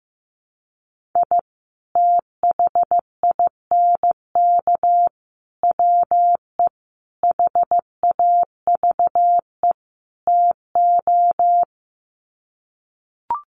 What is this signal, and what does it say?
Morse code, heard as a keyed tone, about 15 wpm, 700 Hz. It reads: I THINK WE HAVE TO
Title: Sets of 5 Words 15wpm